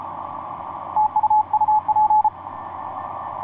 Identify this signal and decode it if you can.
Morse code, heard as a keyed tone, about 33 wpm, 860 Hz. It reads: TU UP
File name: st2ar dl1akz etc 30m cw 21jan2010 1715